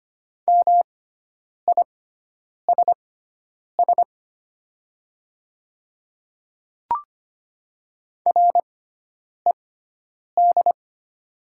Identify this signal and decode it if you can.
Morse code, heard as a keyed tone, about 25 wpm, spaced as for 9 wpm, 700 Hz. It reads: MISS RED
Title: 6x Character Spacing - Top 500 Words - 25wpm